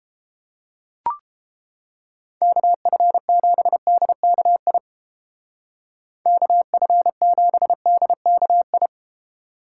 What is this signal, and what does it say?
Morse code, heard as a keyed tone, about 30 wpm, 700 Hz. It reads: KF7DKI KF7DKI